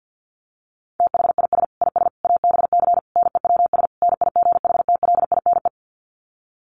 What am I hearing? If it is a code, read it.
Morse code, heard as a keyed tone, about 50 wpm, 700 Hz. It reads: THIS IS ABC NEWS NIGHTLINE
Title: Sets of 5 Words 50wpm